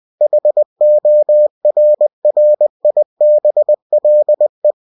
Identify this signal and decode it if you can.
Morse code, heard as a keyed tone, about 20 wpm, 600 Hz. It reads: HORRIBLE